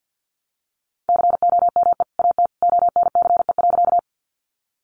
Morse code, heard as a keyed tone, about 50 wpm, 700 Hz. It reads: COME AT ONCE.
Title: Sentences from Top 300 Words 50wpm